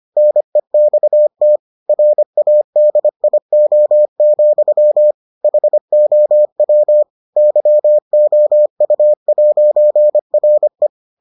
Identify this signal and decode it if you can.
Morse code, heard as a keyed tone, about 25 wpm, 600 Hz. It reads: NEXT RADIO, HOW YOU'RE